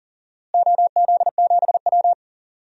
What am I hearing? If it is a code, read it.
Morse code, heard as a keyed tone, about 40 wpm, 700 Hz. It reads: OZ7W
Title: DXCC-100 Call Signs - 4 Characters - Repeated 3x - 40wpm